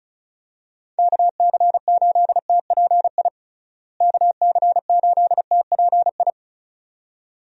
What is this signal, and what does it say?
Morse code, heard as a keyed tone, about 36 wpm, 700 Hz. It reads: KC8TPI KC8TPI
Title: USA Call Signs - 6 Characters - Repeated 2x - 35wpm